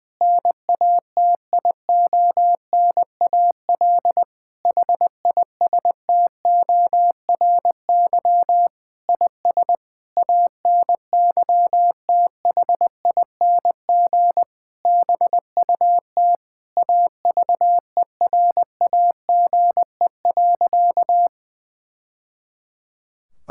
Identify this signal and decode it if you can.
Morse code, heard as a keyed tone, about 21 wpm, 700 Hz. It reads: NATIONAL HISTORY IS ANYTHING BUT AVERAGE.
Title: Sentences from Top 1000 Words - Review Part 6 - 20wpm